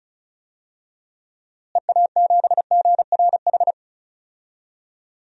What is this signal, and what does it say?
Morse code, heard as a keyed tone, about 36 wpm, 700 Hz. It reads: EA7GRH
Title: DXCC-100 Call Signs - 6 Characters - Repeated 2x - 35wpm